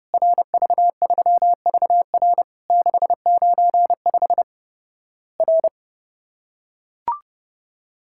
Morse code, heard as a keyed tone, about 30 wpm, 700 Hz. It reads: RV3VR 695 R
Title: Worked All Europe DX Contest - QTC Exchange - 30wpm